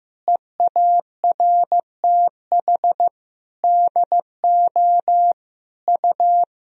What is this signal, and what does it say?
Morse code, heard as a keyed tone, about 15 wpm, 700 Hz. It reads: EARTH DO U
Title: Sentences from Top 200 Words and Abbreviations 15wpm